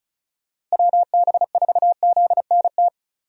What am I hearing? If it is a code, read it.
Morse code, heard as a keyed tone, about 36 wpm, 700 Hz. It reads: WB4ZNT
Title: USA Call Signs - 6 Characters - Repeated 2x - 35wpm